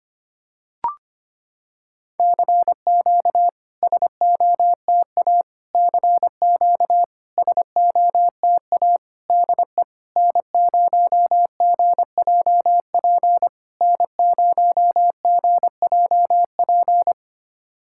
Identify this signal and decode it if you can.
Morse code, heard as a keyed tone, about 25 wpm, 700 Hz. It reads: CQ SOTA CQ SOTA DE N0GJP N0GJP
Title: SOTA QSO Practice 25wpm